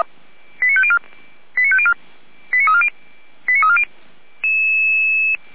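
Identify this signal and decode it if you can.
Five-tone selective call (ZVEI), heard as quick runs of five tones, tones of about 70 ms. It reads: 89504 89504 89340 89340
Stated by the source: ZVEI five-tone selective call 89135 received by SDR.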